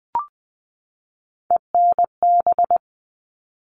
Morse code, heard as a keyed tone, about 20 wpm, 700 Hz. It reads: ENB
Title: All Cut Numbers - 3-Digit 20wpm